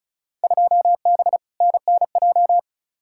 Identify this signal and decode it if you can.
Morse code, heard as a keyed tone, about 36 wpm, 700 Hz. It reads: 2B NNJ
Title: Field Day Exchanges Encore - 35wpm